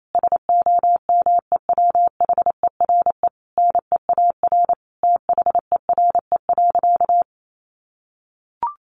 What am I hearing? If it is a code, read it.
Morse code, heard as a keyed tone, about 28 wpm, 700 Hz. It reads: SOMEWHERE NEAR THERE.
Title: Sentences from Top 1400 Words 28wpm